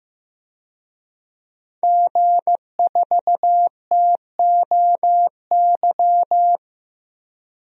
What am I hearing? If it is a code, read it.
Morse code, heard as a keyed tone, about 15 wpm, 700 Hz. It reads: G4TOY
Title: DXCC-100 Call Signs - 5 Characters - 15wpm